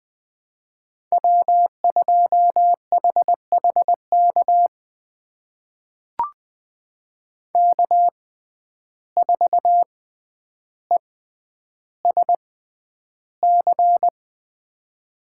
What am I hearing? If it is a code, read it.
Morse code, heard as a keyed tone, about 20 wpm, spaced as for 10 wpm, 700 Hz. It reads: W2HHK K4ESC
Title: USA Call Signs - 5 Characters - 6x Character Spacing - 20wpm